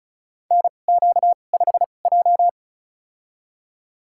Morse code, heard as a keyed tone, about 36 wpm, spaced as for 25 wpm, 700 Hz. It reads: NQ5J
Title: USA Call Signs - 4 Characters - 2x Character Spacing - 35wpm